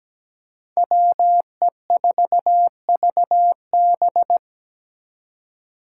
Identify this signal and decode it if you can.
Morse code, heard as a keyed tone, about 20 wpm, 700 Hz. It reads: WE4VB
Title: Speed Racing - USA 2x2 Call Signs 17wpm